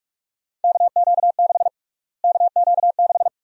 Morse code, heard as a keyed tone, about 46 wpm, 690 Hz. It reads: KQ6 KQ6
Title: USA Call Sign Prefixes - Repeated 2x - 45wpm